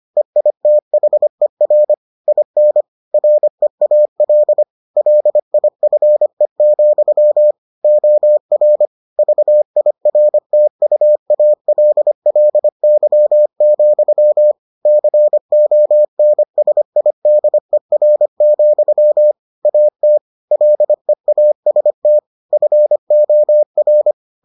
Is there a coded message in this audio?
Yes: Morse code from 0.2 to 24.1 s, heard as a keyed tone, about 25 wpm, 600 Hz. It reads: EITHER IN REAL LIFE, OR VIRTUALLY, CONSIDER, AT LEAST FOR